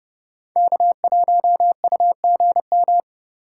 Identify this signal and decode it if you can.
Morse code, heard as a keyed tone, about 31 wpm, 700 Hz. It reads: K1UGM
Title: USA Call Signs - 5 Characters - 4x Character Spacing - 30wpm